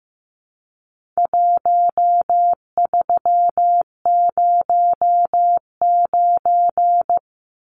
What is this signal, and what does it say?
Morse code, heard as a keyed tone, about 15 wpm, 700 Hz. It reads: 1309
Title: Numbers - 4-Digit 15wpm